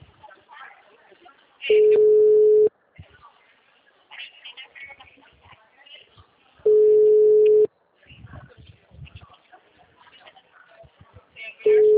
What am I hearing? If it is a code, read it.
Ringback tone (Europe, 425 Hz), with speech.